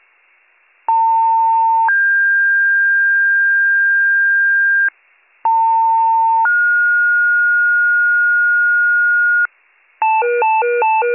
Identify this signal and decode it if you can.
Two-tone page, heard as a one-second tone, then a three-second tone, about 900 then 1600 Hz, about 900 then 1430 Hz, with noise. It is heard from 0.9 to 9.5 s.